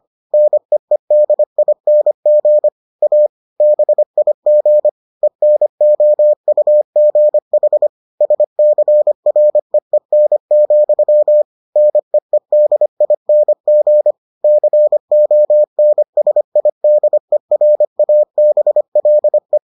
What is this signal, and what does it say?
Morse code, heard as a keyed tone, about 25 wpm, 600 Hz. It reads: NEEDING A BIG ENOUGH SCREEN, NEEDING CONSIDERABLE